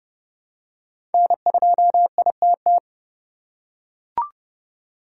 Morse code, heard as a keyed tone, about 30 wpm, 700 Hz. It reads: N2ITT
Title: USA Call Signs - 5 Characters - 3x Character Spacing - 30wpm